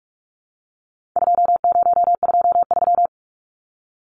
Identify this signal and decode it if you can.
Morse code, heard as a keyed tone, about 44 wpm, 700 Hz. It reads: 2023
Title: Numbers - 4-Digit 45wpm